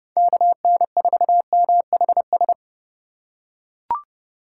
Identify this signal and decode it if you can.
Morse code, heard as a keyed tone, about 30 wpm, 700 Hz. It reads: KN4MHS
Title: USA Call Signs - 2x Character Spacing - 30wpm